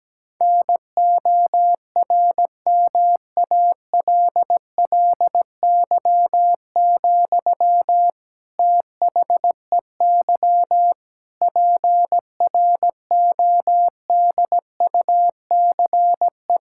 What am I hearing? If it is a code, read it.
Morse code, heard as a keyed tone, about 17 wpm, 700 Hz. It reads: NORMALLY, THEY PRODUCE
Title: Sentences from Top 1000 Words - Review Part 6 - 17wpm